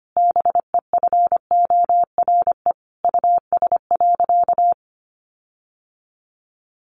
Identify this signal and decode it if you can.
Morse code, heard as a keyed tone, about 25 wpm, 700 Hz. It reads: BEFORE US.
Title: Sentences from Top 1300 Words 25wpm